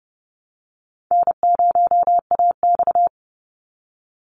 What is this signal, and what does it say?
Morse code, heard as a keyed tone, about 30 wpm, 700 Hz. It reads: N0AX
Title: Easy Call Signs Repeated 3x - 30wpm